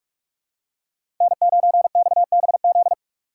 Morse code, heard as a keyed tone, about 45 wpm, 700 Hz. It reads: N9XBZ
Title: USA Call Signs - 2x Character Spacing - 45wpm